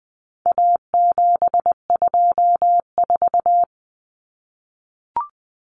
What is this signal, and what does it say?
Morse code, heard as a keyed tone, about 20 wpm, 700 Hz. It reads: A724
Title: Common Cut Numbers - 4-Digit 20wpm